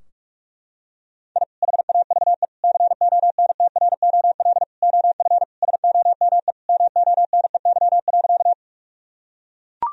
Morse code, heard as a keyed tone, about 41 wpm, 700 Hz. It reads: I HAVE CONTROL OF SOME MONEY.
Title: Sentences from Top 1000 Words - Review Part 3 - 45wpm